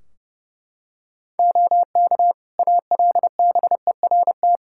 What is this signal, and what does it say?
Morse code, heard as a keyed tone, about 32 wpm, 700 Hz. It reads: OK ALBERT
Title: Practice QSO - Exchange 3 - More Info - 30wpm